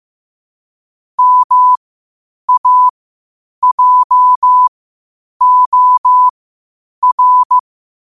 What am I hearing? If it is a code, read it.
Morse code, heard as a keyed tone, about 15 wpm, spaced as for 8 wpm, 1000 Hz. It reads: MAJOR
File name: qotdfiles / 8wpm